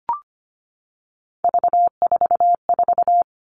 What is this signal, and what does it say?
Morse code, heard as a keyed tone, about 25 wpm, 700 Hz. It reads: V44
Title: Three Challenging Letter-Number Combinations 25wpm